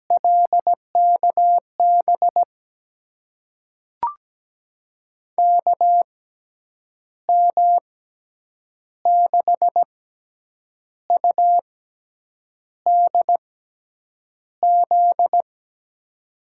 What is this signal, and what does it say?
Morse code, heard as a keyed tone, about 17 wpm, spaced as for 7 wpm, 700 Hz. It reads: LKB KM6UDZ
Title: USA Call Signs - 6x Character Spacing - 17wpm